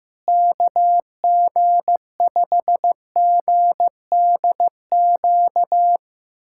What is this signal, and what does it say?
Morse code, heard as a keyed tone, about 16 wpm, 700 Hz. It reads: KG5GDQ
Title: USA Callsign DE Callsign 15wpm